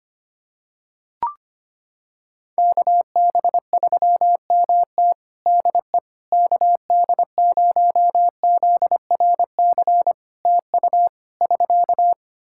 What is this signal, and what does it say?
Morse code, heard as a keyed tone, about 25 wpm, 700 Hz. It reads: KB3MT DE KD0ZRC TU <SK>
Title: Practice QSO - Exchange 4 - Ending - 25wpm